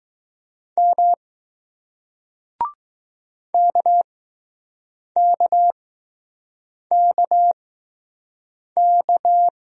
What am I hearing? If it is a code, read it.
Morse code, heard as a keyed tone, about 20 wpm, spaced as for 6 wpm, 700 Hz. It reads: M KKKK